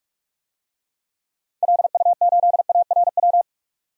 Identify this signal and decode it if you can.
Morse code, heard as a keyed tone, about 46 wpm, 700 Hz. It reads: LU8ARW
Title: DXCC-100 Call Signs - 6 Characters - 45wpm